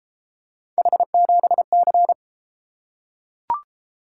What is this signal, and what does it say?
Morse code, heard as a keyed tone, about 31 wpm, 700 Hz. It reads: H7C